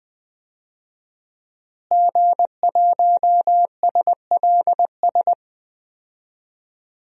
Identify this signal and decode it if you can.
Morse code, heard as a keyed tone, about 20 wpm, 700 Hz. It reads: G1SLS
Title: DXCC-100 Call Signs - 5 Characters - Repeated 2x - 20wpm